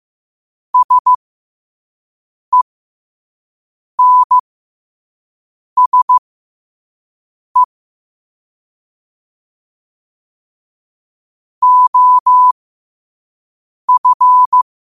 Morse code, heard as a keyed tone, about 15 wpm, spaced as for 5 wpm, 1000 Hz. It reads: SENSE OF
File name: qotdfiles / 5wpm